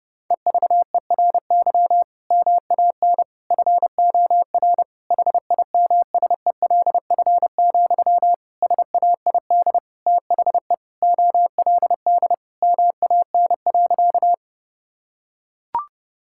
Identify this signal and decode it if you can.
Morse code, heard as a keyed tone, about 30 wpm, 700 Hz. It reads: EVERY MAN FOR HIMSELF, SAID THE OLD MAN.
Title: Sentences from Top 1600 Words 30wpm